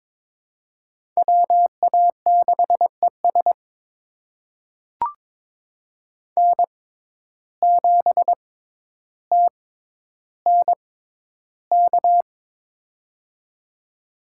Morse code, heard as a keyed tone, about 21 wpm, spaced as for 11 wpm, 700 Hz. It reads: WA6ES N7TNK
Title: USA Call Signs - 6x Character Spacing - 22wpm